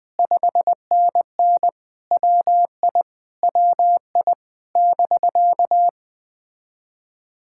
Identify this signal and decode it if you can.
Morse code, heard as a keyed tone, about 21 wpm, 700 Hz. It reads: 5NN WI WI <BK>
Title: POTA QSO Practice - 20wpm